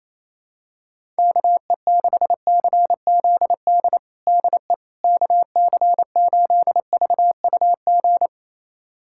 Morse code, heard as a keyed tone, about 28 wpm, 700 Hz. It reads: KE6CZD DE KC8VUG
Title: USA Callsign DE Callsign - 6 Characters - 28wpm